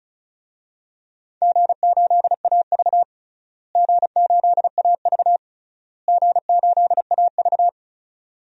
Morse code, heard as a keyed tone, about 36 wpm, 700 Hz. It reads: G8AV G8AV G8AV
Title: DXCC-100 Call Signs - 4 Characters - Repeated 3x - 35wpm